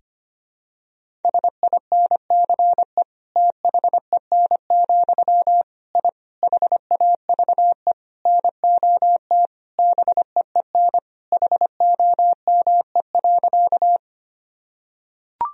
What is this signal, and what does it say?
Morse code, heard as a keyed tone, about 25 wpm, 700 Hz. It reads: SINCE THEN, I HAVE NOT BEEN HOME.
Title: Sentences from Top 1000 Words - Review Part 1 - 25wpm